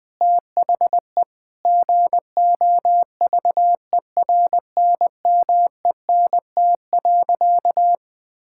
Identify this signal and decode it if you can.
Morse code, heard as a keyed tone, about 20 wpm, 700 Hz. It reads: THE GOVERNMENT.
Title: Sentences from Top 1000 Words - Review Part 6 - 20wpm